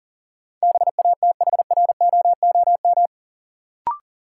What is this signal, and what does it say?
Morse code, heard as a keyed tone, about 40 wpm, 700 Hz. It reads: BATHROOM
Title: Sets of 5 Words Encore - New Word Drill - 40wpm